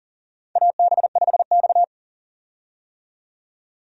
Morse code, heard as a keyed tone, about 40 wpm, 700 Hz. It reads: AB5X
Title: USA Call Signs - 4 Characters - 40wpm